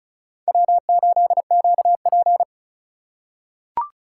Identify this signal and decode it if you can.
Morse code, heard as a keyed tone, about 36 wpm, 700 Hz. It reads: W8QP
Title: USA Call Signs - 4 Characters - 2x Character Spacing - 35wpm